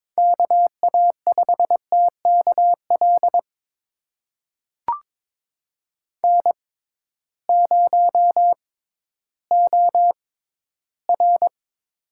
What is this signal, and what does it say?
Morse code, heard as a keyed tone, about 22 wpm, spaced as for 13 wpm, 700 Hz. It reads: KA5TKL N0OR
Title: USA Call Signs - 6x Character Spacing - 22wpm